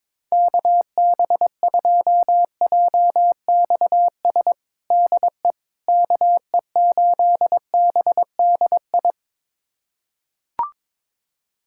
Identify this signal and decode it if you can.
Morse code, heard as a keyed tone, about 22 wpm, 700 Hz. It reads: KB2JXS DE KE8BDI
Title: USA Callsign DE Callsign 22wpm